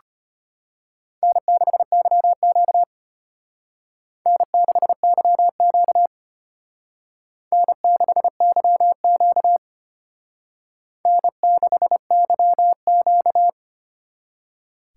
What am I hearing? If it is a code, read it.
Morse code, heard as a keyed tone, about 32 wpm, 700 Hz. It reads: N6YQ N6YQ N6YQ N6YQ